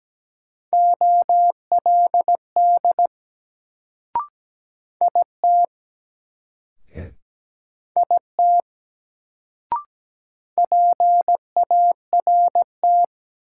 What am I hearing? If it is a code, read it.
Morse code, heard as a keyed tone, about 17 wpm, 700 Hz. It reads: OLD IT IT PART
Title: Sets of 2 Words - New Word Drill - 17wpm